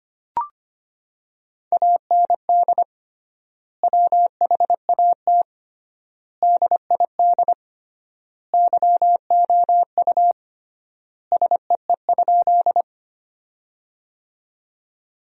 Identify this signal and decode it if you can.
Morse code, heard as a keyed tone, about 25 wpm, 700 Hz. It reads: AND WHAT DID YOU SEE?
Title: 3x Word-Spacing - Sentences from Top 100 Words - 25wpm